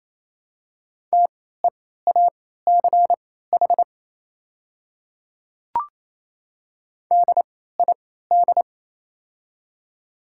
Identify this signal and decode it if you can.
Morse code, heard as a keyed tone, about 28 wpm, spaced as for 16 wpm, 700 Hz. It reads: TEACH DID
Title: 3x Character Spacing - Top 500 Words - 28wpm